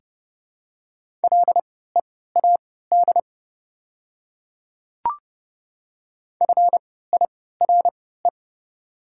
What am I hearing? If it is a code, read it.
Morse code, heard as a keyed tone, about 30 wpm, spaced as for 17 wpm, 700 Hz. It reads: LEAD FIRE
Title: 3x Character Spacing - Top 500 Words - 30wpm